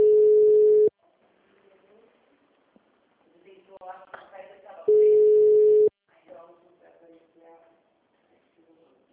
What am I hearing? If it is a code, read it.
Ringback tone (Europe, 425 Hz), with speech.